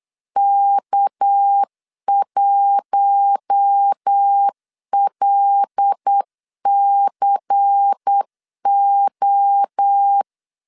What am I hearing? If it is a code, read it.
Morse code, heard as a keyed tone, about 8 wpm, 800 Hz. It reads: K1LCO
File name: LEZIONI / 6 PKX0 / 6 PKX 8wpm 3041